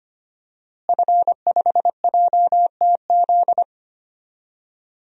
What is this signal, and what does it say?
Morse code, heard as a keyed tone, about 25 wpm, 700 Hz. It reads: F5JTZ
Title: DXCC-100 Call Signs - 5 Characters - Repeated 2x - 25wpm